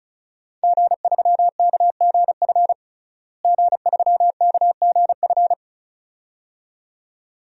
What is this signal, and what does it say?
Morse code, heard as a keyed tone, about 34 wpm, 700 Hz. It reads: G3KGF G3KGF
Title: DXCC-100 Call Signs - 5 Characters - Repeated 2x - 35wpm